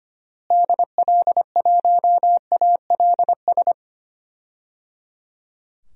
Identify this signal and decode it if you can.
Morse code, heard as a keyed tone, about 25 wpm, 700 Hz. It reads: DL1ALS
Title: DXCC-100 Call Signs - 6 Characters - Repeated 3x - 25wpm